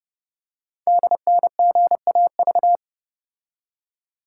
Morse code, heard as a keyed tone, about 30 wpm, 700 Hz. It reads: DNGAV
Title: Code Groups - 5 Letters - 30wpm